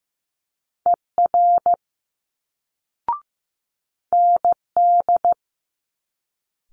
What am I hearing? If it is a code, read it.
Morse code, heard as a keyed tone, about 15 wpm, 700 Hz. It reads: ER ND
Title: Multi-letter Phonemes Quick Fire 15wpm